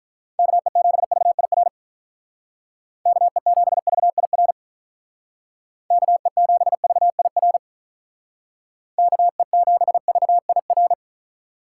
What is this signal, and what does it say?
Morse code, heard as a keyed tone, about 45 wpm, 700 Hz. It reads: KE7VIR KE7VIR KE7VIR KE7VIR